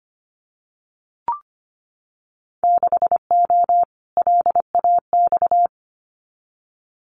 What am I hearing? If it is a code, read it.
Morse code, heard as a keyed tone, about 25 wpm, 700 Hz. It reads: 6O LAX
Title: Winter Field Day Exchanges Encore 25wpm